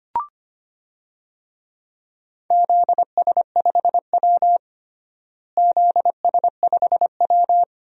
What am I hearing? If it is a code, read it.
Morse code, heard as a keyed tone, about 25 wpm, 700 Hz. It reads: ZS5W ZS5W
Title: DXCC-100 Call Signs - 4 Characters - Repeated 2x - 25wpm